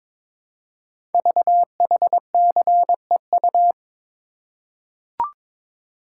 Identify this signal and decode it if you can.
Morse code, heard as a keyed tone, about 23 wpm, 700 Hz. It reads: VHCEU
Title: Code Groups - 5 Letters - 22wpm